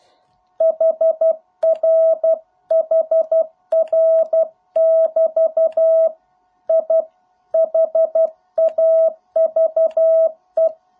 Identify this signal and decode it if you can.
Morse code, heard as a keyed tone, about 12 wpm, 650 Hz. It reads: HR HR <BT> I HAVE